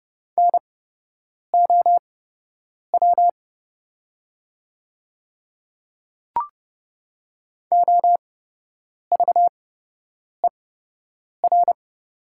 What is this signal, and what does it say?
Morse code, heard as a keyed tone, about 30 wpm, spaced as for 8 wpm, 700 Hz. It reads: NOW OVER
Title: 8x Character Spacing - Top 100 Words - 30wpm